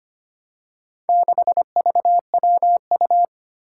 Morse code, heard as a keyed tone, about 25 wpm, 700 Hz. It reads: 6VWU
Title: Four Challenging Letter-Number Combinations 25wpm